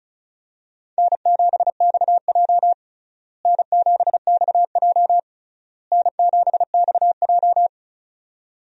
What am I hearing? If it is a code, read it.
Morse code, heard as a keyed tone, about 36 wpm, 700 Hz. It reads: N7XJ N7XJ N7XJ
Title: USA Call Signs - 4 Characters - Repeated 3x - 35wpm